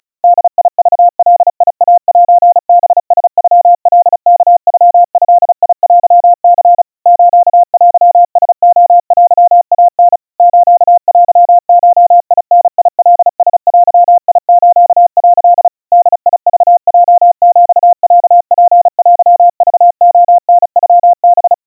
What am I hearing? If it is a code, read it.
Morse code (Japanese Wabun code), heard as a keyed tone, about 34 wpm, 700 Hz. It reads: ボクガイセハラノカワノドテニ ステラレテイタ ステゴダカラデス」 ボクヲヒロツテクレタノハ
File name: kawa / Morse sample / kamyu 130 cpm 700 Hz Crispy